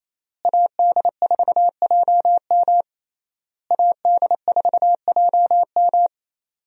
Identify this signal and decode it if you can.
Morse code, heard as a keyed tone, about 28 wpm, 700 Hz. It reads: AD4JM AD4JM
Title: USA Call Signs - 5 Characters - Repeated 3x - 28wpm